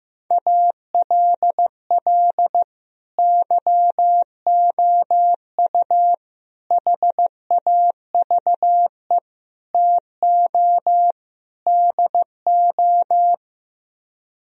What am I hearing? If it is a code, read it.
Morse code, heard as a keyed tone, about 15 wpm, 700 Hz. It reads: ALL YOU HAVE TO DO
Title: Speed-Racing - Sets of 5 Words 15wpm